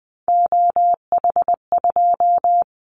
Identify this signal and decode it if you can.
Morse code, heard as a keyed tone, about 20 wpm, 700 Hz. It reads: OH2
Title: Callsign Prefixes for Top 50 DXCC - 20wpm